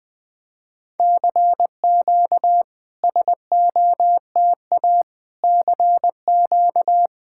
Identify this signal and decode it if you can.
Morse code, heard as a keyed tone, about 21 wpm, 690 Hz. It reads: CQ SOTA CQ
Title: SOTA QSO Practice 20wpm